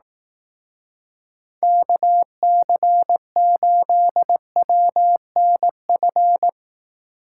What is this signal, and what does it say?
Morse code, heard as a keyed tone, about 18 wpm, 700 Hz. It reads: KC8WNF
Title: Speed Racing - USA 2x3 Call Signs 15wpm